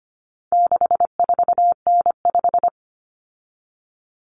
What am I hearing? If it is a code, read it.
Morse code, heard as a keyed tone, about 25 wpm, 700 Hz. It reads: 64N5
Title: Common Cut Numbers - 4-Digit 25wpm